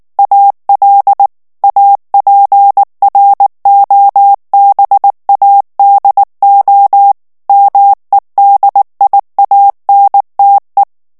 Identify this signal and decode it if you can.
Morse code, heard as a keyed tone, about 19 wpm, 800 Hz. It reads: AL APROBADO MEDIANTE
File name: morse mp3 / ESPAGNOL TEXTE EN CLAIR 2 VITESSE 18 MOTS MINUTES